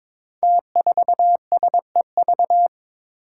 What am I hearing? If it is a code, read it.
Morse code, heard as a keyed tone, about 22 wpm, 700 Hz. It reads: T4SEV
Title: Five Challenging Letter-Number Combinations 22wpm